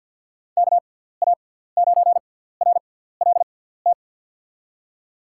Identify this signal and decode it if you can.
Morse code, heard as a keyed tone, about 50 wpm, spaced as for 17 wpm, 700 Hz. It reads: KA9RLT
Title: USA Call Signs - 6x Character Spacing - 50wpm